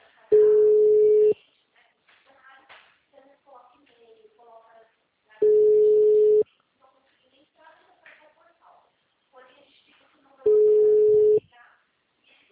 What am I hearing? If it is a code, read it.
Ringback tone (Europe, 425 Hz), with speech.